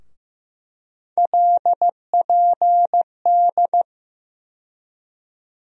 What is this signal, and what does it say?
Morse code, heard as a keyed tone, about 15 wpm, 700 Hz. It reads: LPD
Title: USA Call Sign Postfixes - 3 Letters 15wpm